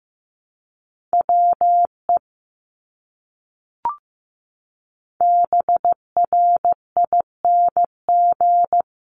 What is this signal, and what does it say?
Morse code, heard as a keyed tone, about 15 wpm, 700 Hz. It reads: WE BRING
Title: Top 500 Words Repeated 15wpm